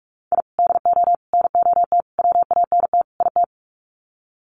Morse code, heard as a keyed tone, about 44 wpm, 690 Hz. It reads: I DO NOT WANT IT
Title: Sets of 5 Words Encore2 45wpm